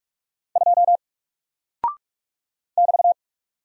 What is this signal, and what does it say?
Morse code, heard as a keyed tone, about 44 wpm, 700 Hz. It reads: 2 <BT>
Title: Single Character (Letter:Number:Punctuation:Prosign) Alt 45wpm